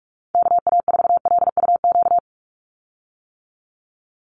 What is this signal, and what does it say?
Morse code, heard as a keyed tone, about 44 wpm, 700 Hz. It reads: KA4LUQ
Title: USA Call Signs 45wpm